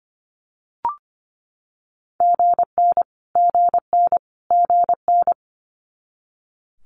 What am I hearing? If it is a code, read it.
Morse code, heard as a keyed tone, about 25 wpm, 700 Hz. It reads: GN GN GN
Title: Multi-letter Phonemes Repeated 3x 25wpm